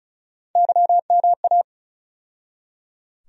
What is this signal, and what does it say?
Morse code, heard as a keyed tone, about 36 wpm, 700 Hz. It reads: YMA
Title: USA Call Sign Postfixes - 3 Letters 35wpm